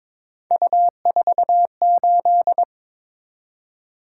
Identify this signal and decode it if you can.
Morse code, heard as a keyed tone, about 26 wpm, 700 Hz. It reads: U48
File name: Speed-Racing - Three Letter-Number Combinations - 22wpm